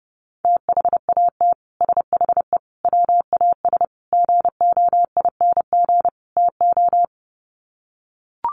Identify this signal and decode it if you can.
Morse code, heard as a keyed tone, about 30 wpm, 700 Hz. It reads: THAT SHE WAS GOING TO
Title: Sets of 5 Words Encore2 30wpm